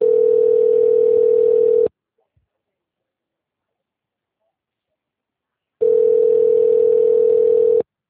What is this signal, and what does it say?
Ringback tone (North America), 440 + 480 Hz.